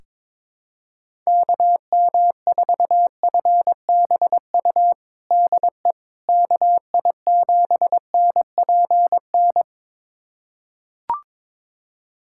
Morse code, heard as a keyed tone, about 22 wpm, 700 Hz. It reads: KM4FBU DE KI7NPN
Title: USA Callsign DE Callsign 22wpm